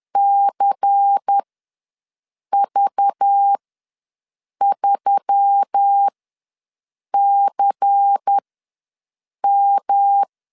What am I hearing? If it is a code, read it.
Morse code, heard as a keyed tone, about 10 wpm, spaced as for 6 wpm, 800 Hz. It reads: CV3CM
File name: LEZIONI / 5 LBVC / 5 LBVC ws 10wpm 3733